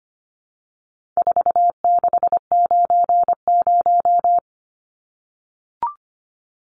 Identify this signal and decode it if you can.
Morse code, heard as a keyed tone, about 25 wpm, 700 Hz. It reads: 4690
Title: Numbers - 4-Digit 25wpm